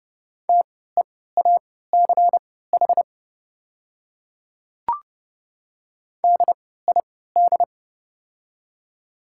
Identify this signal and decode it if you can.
Morse code, heard as a keyed tone, about 30 wpm, spaced as for 17 wpm, 700 Hz. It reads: TEACH DID
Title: 3x Character Spacing - Top 500 Words - 30wpm